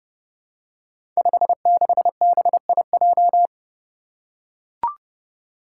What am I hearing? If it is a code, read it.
Morse code, heard as a keyed tone, about 30 wpm, 700 Hz. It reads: 56BIJ
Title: Five Challenging Letter-Number Combinations 30wpm